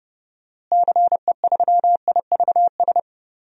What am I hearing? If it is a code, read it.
Morse code, heard as a keyed tone, about 31 wpm, 700 Hz. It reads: CE3IVS
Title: DXCC-100 Call Signs - 6 Characters - Repeated 2x - 30wpm